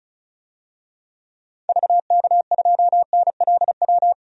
Morse code, heard as a keyed tone, about 34 wpm, 700 Hz. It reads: VK2NLW
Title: DXCC-100 Call Signs - 6 Characters - 35wpm